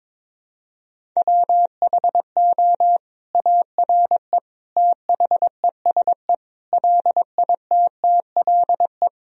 Morse code, heard as a keyed tone, about 22 wpm, 700 Hz. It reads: WHO ARE THESE LITTLE
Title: Sentences from Top 1000 Words - Review Part 1 - 22wpm